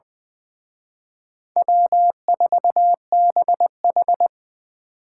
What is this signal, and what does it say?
Morse code, heard as a keyed tone, about 24 wpm, 700 Hz. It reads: W4BH
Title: Speed Racing - USA 1x2 Call Signs 20wpm